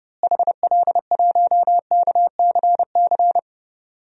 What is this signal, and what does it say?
Morse code, heard as a keyed tone, about 30 wpm, 700 Hz. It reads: HL1KCC
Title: DXCC-100 Call Signs - 6 Characters - Repeated 3x - 30wpm